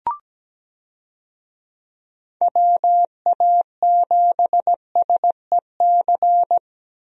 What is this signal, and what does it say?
Morse code, heard as a keyed tone, about 17 wpm, 700 Hz. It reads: WA7SEC
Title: USA Call Signs - 6 Characters - Repeated 3x - 17wpm